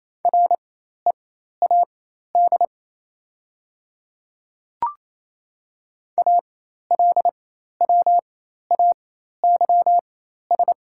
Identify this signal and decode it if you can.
Morse code, heard as a keyed tone, about 27 wpm, spaced as for 13 wpm, 700 Hz. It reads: READ ALWAYS
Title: 4x Character Spacing - Top 300 Words - 28wpm